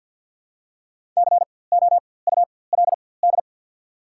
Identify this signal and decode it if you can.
Morse code, heard as a keyed tone, about 54 wpm, spaced as for 24 wpm, 690 Hz. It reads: COULD
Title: 4x Character Spacing - Top 100 Words - 50wpm